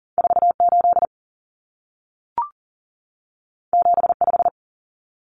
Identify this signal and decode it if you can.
Morse code, heard as a keyed tone, about 40 wpm, 700 Hz. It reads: 4 8 7 5
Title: Common Cut Numbers - 2-Digit 40wpm